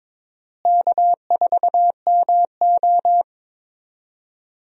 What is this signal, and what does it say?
Morse code, heard as a keyed tone, about 22 wpm, 700 Hz. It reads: K4MO
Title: USA Call Signs - 4 Characters - Repeated 3x - 22wpm